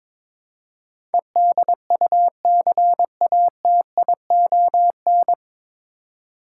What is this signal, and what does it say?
Morse code, heard as a keyed tone, about 21 wpm, 700 Hz. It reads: EDUCATION
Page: Morse Code Ninja practice 22wpm